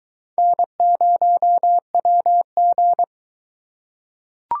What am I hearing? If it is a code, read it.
Morse code, heard as a keyed tone, about 23 wpm, 700 Hz. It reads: N0WG